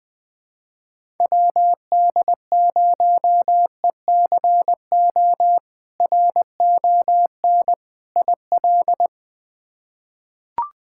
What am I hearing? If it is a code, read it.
Morse code, heard as a keyed tone, about 20 wpm, 700 Hz. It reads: WD0ECO RON IL
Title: K1USN SST QSO Practice - 20wpm